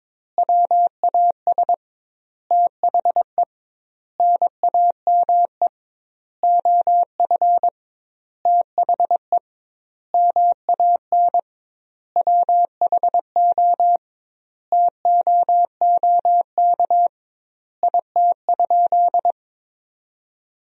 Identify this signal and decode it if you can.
Morse code, heard as a keyed tone, about 22 wpm, 700 Hz. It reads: WAS THE NAME OF THE MAN WHO TOOK IT?
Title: 2x Word-Spacing - Sentences from Top 300 Words - 22wpm